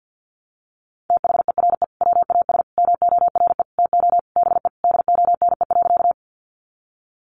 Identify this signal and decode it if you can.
Morse code, heard as a keyed tone, about 50 wpm, 700 Hz. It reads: THERE WAS MORE TO BE DONE.
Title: Sentences from Top 500 Words 50wpm